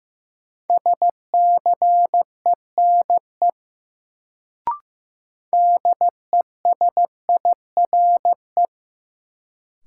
Morse code, heard as a keyed tone, about 15 wpm, 700 Hz. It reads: SCENE DESIRE
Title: Sentences from Top 1100 Words - New Word Drill Alt - 15wpm